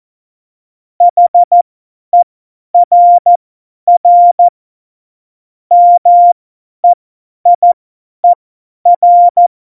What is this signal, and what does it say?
Morse code, heard as a keyed tone, about 14 wpm, spaced as for 10 wpm, 700 Hz. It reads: HERR MEIER
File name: Morsekurs / Witze / Witz 14 Juli 2020, 50 F70